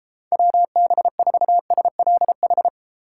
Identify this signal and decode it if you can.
Morse code, heard as a keyed tone, about 29 wpm, 700 Hz. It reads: WB4SLH